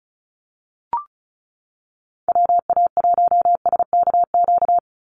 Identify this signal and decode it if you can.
Morse code, heard as a keyed tone, about 36 wpm, 700 Hz. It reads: WA1SKQ
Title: Call Signs - 35wpm